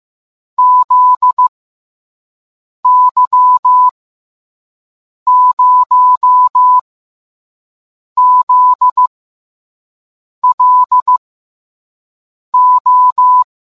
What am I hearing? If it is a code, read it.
Morse code, heard as a keyed tone, about 15 wpm, 1000 Hz. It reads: Z Y 0 Z L O